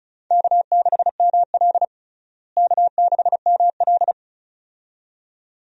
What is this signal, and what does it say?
Morse code, heard as a keyed tone, about 36 wpm, 700 Hz. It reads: K6ML K6ML
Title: USA Call Signs - 4 Characters - Repeated 2x - 35wpm